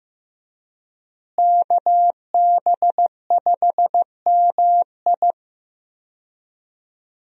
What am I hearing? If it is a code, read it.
Morse code, heard as a keyed tone, about 15 wpm, 700 Hz. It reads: KB5MI
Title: USA Call Signs - 5 Characters - Repeated 3x - 15wpm